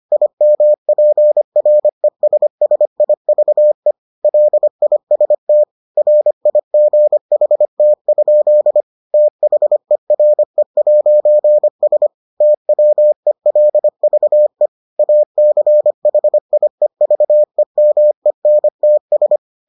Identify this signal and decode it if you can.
Morse code, heard as a keyed tone, about 25 wpm, 600 Hz. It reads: IMPRESSIVE LIST RIGHT? THERE'S TWELVE ACHIEVEMENTS